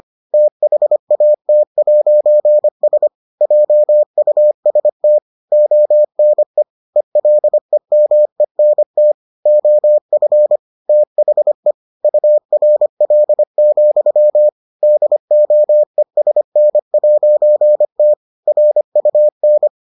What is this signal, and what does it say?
Morse code, heard as a keyed tone, about 25 wpm, 600 Hz. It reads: THAT'S JUST ONE ELEMENT OF THE URL, DOESN'T RUN